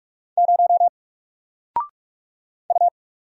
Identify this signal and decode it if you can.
Morse code, heard as a keyed tone, about 44 wpm, 700 Hz. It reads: 0 U
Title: Single Letter-Number Alt 45wpm